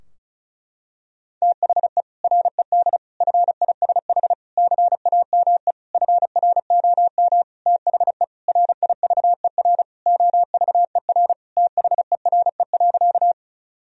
Morse code, heard as a keyed tone, about 34 wpm, 700 Hz. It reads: THE RED FISH CAME FROM THE RIVER OVER THERE.
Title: Sentences from Top 1000 Words - Review Part 2 - 35wpm